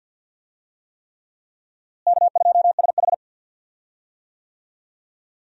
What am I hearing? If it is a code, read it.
Morse code, heard as a keyed tone, about 47 wpm, 700 Hz. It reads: K2SH
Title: USA Call Signs - 4 Characters - Repeated 3x - 50wpm